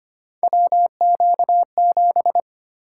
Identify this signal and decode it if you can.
Morse code, heard as a keyed tone, about 25 wpm, 700 Hz. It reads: WQ7
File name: Three Letter-Number Combinations - 25wpm